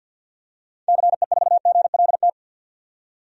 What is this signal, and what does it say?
Morse code, heard as a keyed tone, about 52 wpm, 700 Hz. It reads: CE4GLT
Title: DXCC-100 Call Signs - 6 Characters - 50wpm